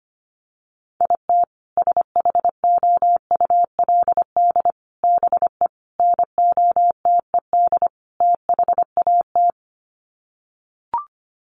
Morse code, heard as a keyed tone, about 25 wpm, 700 Hz. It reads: IT SHOULD BE NOTED THAT
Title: Sets of 5 Words 25wpm